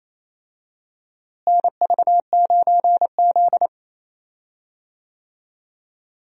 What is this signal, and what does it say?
Morse code, heard as a keyed tone, about 27 wpm, 700 Hz. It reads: NV9Z
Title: USA Call Signs - 4 Characters - Repeated 2x - 28wpm